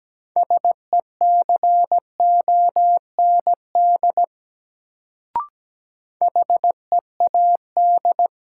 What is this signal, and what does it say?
Morse code, heard as a keyed tone, about 17 wpm, 700 Hz. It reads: SECOND HEAD
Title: Sentences from Top 300 Words - New Word Drill - 17wpm